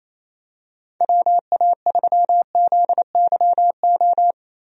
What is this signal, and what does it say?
Morse code, heard as a keyed tone, about 28 wpm, 700 Hz. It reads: WA3ZYO
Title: USA Call Signs - 6 Characters - 28wpm